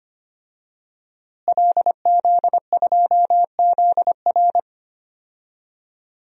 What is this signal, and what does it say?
Morse code, heard as a keyed tone, about 25 wpm, 700 Hz. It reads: LZ2ZR
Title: DXCC-100 Call Signs - 5 Characters - Repeated 3x - 25wpm